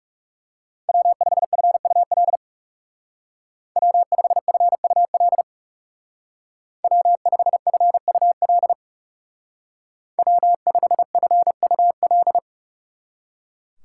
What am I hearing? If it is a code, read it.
Morse code, heard as a keyed tone, about 36 wpm, 700 Hz. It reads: W5FUL W5FUL W5FUL W5FUL
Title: Speed Racing - USA 1x3 Call Signs 30wpm